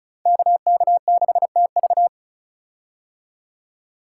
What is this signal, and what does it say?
Morse code, heard as a keyed tone, about 36 wpm, 700 Hz. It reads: KK6TV
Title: K1USN SST QSO Practice - 35wpm